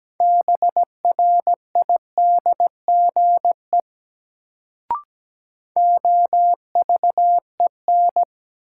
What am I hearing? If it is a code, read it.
Morse code, heard as a keyed tone, about 17 wpm, 700 Hz. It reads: BRIDGE OVEN
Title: Sentences from Top 1200 Words - New Word Drill Alt - 17wpm